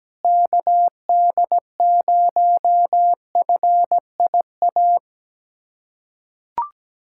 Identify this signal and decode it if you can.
Morse code, heard as a keyed tone, about 17 wpm, 700 Hz. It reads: KD0FIA
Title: USA Callsign DE Callsign 17wpm